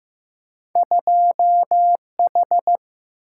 Morse code, heard as a keyed tone, about 15 wpm, 700 Hz. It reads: 2 H
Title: Single-Letter Single-Number 15wpm